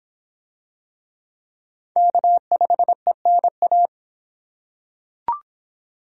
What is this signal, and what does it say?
Morse code, heard as a keyed tone, about 25 wpm, 700 Hz. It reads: K5ENA